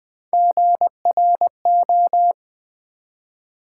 Morse code, heard as a keyed tone, about 20 wpm, 700 Hz. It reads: GRO
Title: Build A Word - 3 Letters At A Time - 6-Letter Words - 20wpm